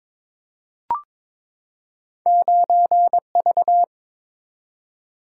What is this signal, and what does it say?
Morse code, heard as a keyed tone, about 22 wpm, 690 Hz. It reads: 9V
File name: practice / 22wpm / Two Letter-Number Combinations - 22wpm